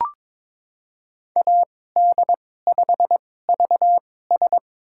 Morse code, heard as a keyed tone, about 22 wpm, spaced as for 16 wpm, 700 Hz. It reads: AD5VS
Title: USA Call Signs - 5 Characters - 2x Character Spacing - 22wpm